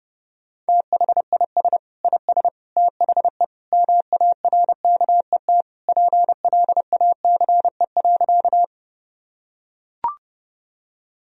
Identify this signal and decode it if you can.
Morse code, heard as a keyed tone, about 30 wpm, 700 Hz. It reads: THIS IS THE MARKET PLACE.
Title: Sentences from Top 1000 Words - Review Part 3 - 30wpm